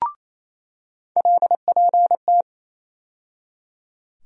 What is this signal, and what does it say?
Morse code, heard as a keyed tone, about 28 wpm, 700 Hz. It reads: LPT
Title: USA Call Sign Postfixes - 3 Letters 28wpm